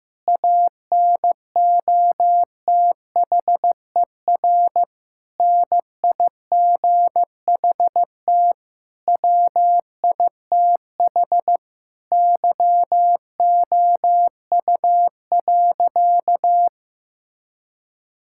Morse code, heard as a keyed tone, about 15 wpm, 700 Hz. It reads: ANOTHER NIGHT WITH YOU.
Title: Sentences from Top 1000 Words - Review Part 5 - 15wpm